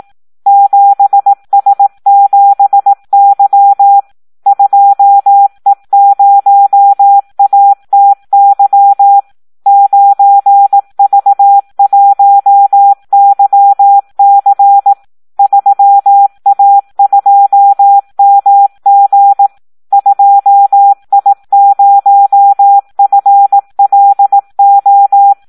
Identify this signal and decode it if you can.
Morse code, heard as a keyed tone, about 18 wpm, 800 Hz. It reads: 7S7Y 2E0ATY 9V1YC 3A2MG 2I0FLO
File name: Audio practica CW / SD 1 18